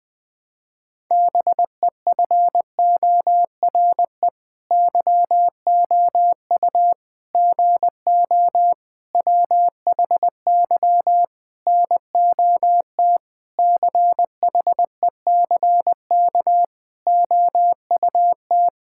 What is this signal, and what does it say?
Morse code, heard as a keyed tone, about 21 wpm, 700 Hz. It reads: BEFORE YOU GO WHY NOT CHECK OUT
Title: Sentences with 9-Word Phrases - 20wpm